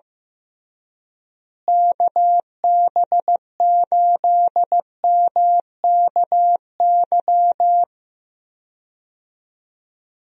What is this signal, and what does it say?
Morse code, heard as a keyed tone, about 15 wpm, 700 Hz. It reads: KB8MKY
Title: USA Call Signs - 6 Characters - Repeated 3x - 15wpm